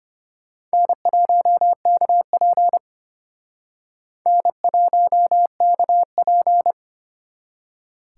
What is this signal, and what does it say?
Morse code, heard as a keyed tone, about 31 wpm, 700 Hz. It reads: N1KP N1KP
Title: Speed Racing - USA 1x2 Call Signs 25wpm